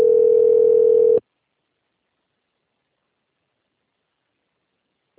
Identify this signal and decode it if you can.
Ringback tone (North America), 440 + 480 Hz.